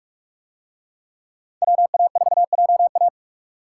Morse code, heard as a keyed tone, about 45 wpm, 700 Hz. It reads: WA4JA